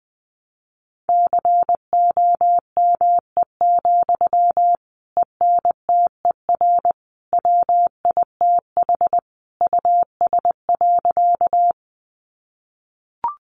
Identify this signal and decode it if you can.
Morse code, heard as a keyed tone, about 20 wpm, 700 Hz. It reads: COME, ENTER WITH US.
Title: Sentences from Top 1300 Words 20wpm